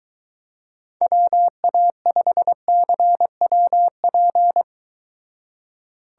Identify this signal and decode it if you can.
Morse code, heard as a keyed tone, about 22 wpm, 700 Hz. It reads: WA5CWP